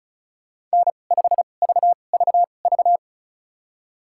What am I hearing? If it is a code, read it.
Morse code, heard as a keyed tone, about 36 wpm, spaced as for 25 wpm, 700 Hz. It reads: N5VVV
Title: USA Call Signs - 5 Characters - 2x Character Spacing - 35wpm